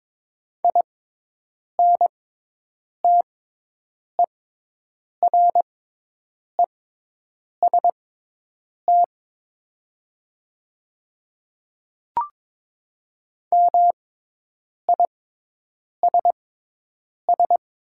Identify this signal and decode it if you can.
Morse code, heard as a keyed tone, about 22 wpm, spaced as for 8 wpm, 700 Hz. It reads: INTEREST MISS
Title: 6x Character Spacing - Top 500 Words - 22wpm